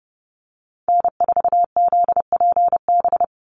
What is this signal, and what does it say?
Morse code, heard as a keyed tone, about 30 wpm, 700 Hz. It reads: N4ZPB
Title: USA Call Signs 30wpm